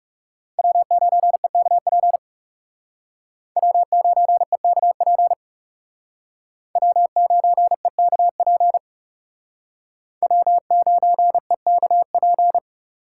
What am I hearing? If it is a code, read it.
Morse code, heard as a keyed tone, about 40 wpm, 700 Hz. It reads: W9EKP W9EKP W9EKP W9EKP